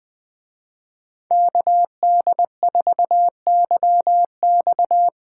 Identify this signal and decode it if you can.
Morse code, heard as a keyed tone, about 20 wpm, 700 Hz. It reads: KD4YX